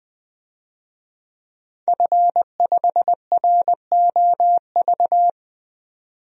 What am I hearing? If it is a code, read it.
Morse code, heard as a keyed tone, about 21 wpm, 700 Hz. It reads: F5ROV
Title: DXCC-100 Call Signs - 5 Characters - 20wpm